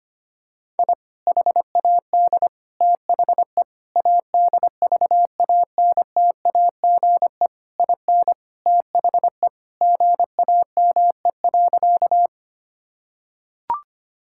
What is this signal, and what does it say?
Morse code, heard as a keyed tone, about 25 wpm, 700 Hz. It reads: I HAD THE ADVANTAGE IN THE GAME.
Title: Sentences from Top 1000 Words - Review Part 6 - 25wpm